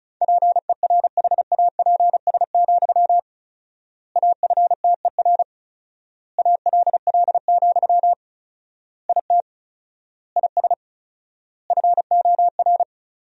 Morse code, heard as a keyed tone, about 36 wpm, 700 Hz. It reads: PERHAPS, AFTER ALL, IT IS FOR
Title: 4x Word-Spacing - Sentences from Top 500 Words - 35wpm